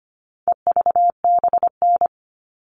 Morse code, heard as a keyed tone, about 25 wpm, 700 Hz. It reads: EVBN
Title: All Cut Numbers - 4-Digit 25wpm